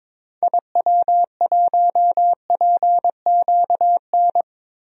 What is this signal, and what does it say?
Morse code, heard as a keyed tone, about 22 wpm, 700 Hz. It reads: IW1PQN
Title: DXCC-100 Call Signs - 6 Characters - Repeated 2x - 22wpm